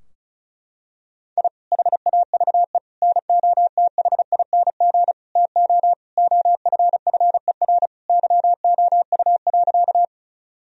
Morse code, heard as a keyed tone, about 36 wpm, 700 Hz. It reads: I HAVE NOTHING TO OFFER YOU.
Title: Sentences from Top 1000 Words - Review Part 4 - 35wpm